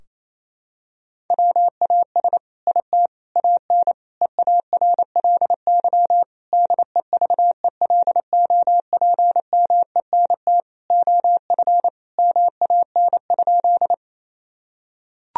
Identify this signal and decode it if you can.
Morse code, heard as a keyed tone, about 28 wpm, 700 Hz. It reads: WAS IT AN EARLY DEVELOPMENT OF MAN?
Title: Sentences from Top 1000 Words - Review Part 4 - 28wpm